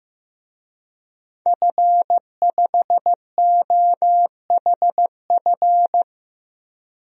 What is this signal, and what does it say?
Morse code, heard as a keyed tone, about 15 wpm, 700 Hz. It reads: F5OHF
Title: DXCC-100 Call Signs - 5 Characters - 15wpm